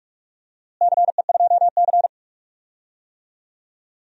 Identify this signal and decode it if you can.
Morse code, heard as a keyed tone, about 46 wpm, 700 Hz. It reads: CE2C
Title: DXCC-100 Call Signs - 4 Characters - Repeated 2x - 45wpm